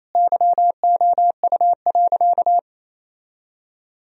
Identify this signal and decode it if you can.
Morse code, heard as a keyed tone, about 28 wpm, 700 Hz. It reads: YOU.
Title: Sentences from Top 1000 Words - Review Part 2 - 28wpm